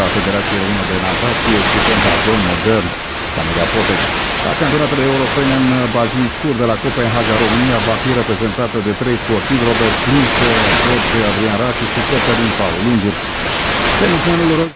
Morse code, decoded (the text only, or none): none